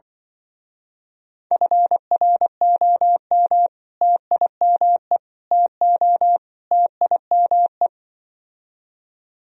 Morse code, heard as a keyed tone, about 24 wpm, 700 Hz. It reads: FROM TIME TO TIME
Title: Speed-Racing - Sets of 4 Words 20wpm